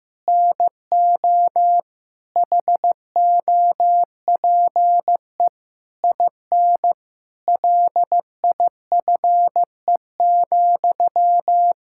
Morse code, heard as a keyed tone, about 15 wpm, 700 Hz. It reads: NO HOPE IN LIFE,
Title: Sentences from Top 1000 Words - Review Part 6 - 15wpm